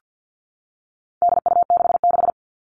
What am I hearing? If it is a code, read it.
Morse code, heard as a keyed tone, about 50 wpm, 700 Hz. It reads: DU66
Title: All Cut Numbers - 4-Digit 50wpm